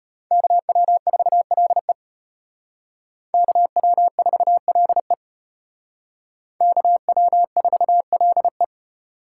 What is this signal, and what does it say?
Morse code, heard as a keyed tone, about 33 wpm, 700 Hz. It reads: KW4LE KW4LE KW4LE